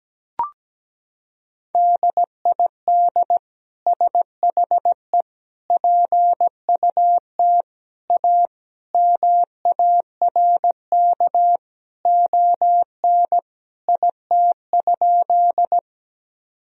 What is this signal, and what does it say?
Morse code, heard as a keyed tone, about 17 wpm, 700 Hz. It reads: DID SHE PUT A MARK ON IT?
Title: Sentences from Top 1000 Words - Review Part 1 - 17wpm